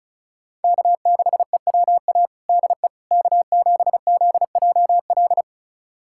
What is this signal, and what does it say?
Morse code, heard as a keyed tone, about 36 wpm, 700 Hz. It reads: K6EWA DE K7ZJL
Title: USA Callsign DE Callsign - 5 Characters - 35wpm